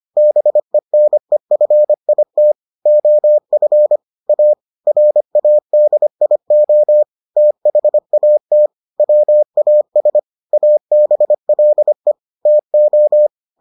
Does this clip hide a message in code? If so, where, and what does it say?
Yes: Morse code from 0.2 to 13.3 s, heard as a keyed tone, about 25 wpm, 600 Hz. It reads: BENEFIT OF A RADIO THAT WAS ABLE TO